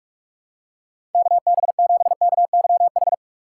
Morse code, heard as a keyed tone, about 46 wpm, 700 Hz. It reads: KB7KYH
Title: USA Call Signs - 2x Character Spacing - 45wpm